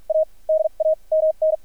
Morse code, heard as a keyed tone, about 53 wpm, spaced as for 38 wpm, 630 Hz. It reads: A DAMN
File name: archived / CRSnet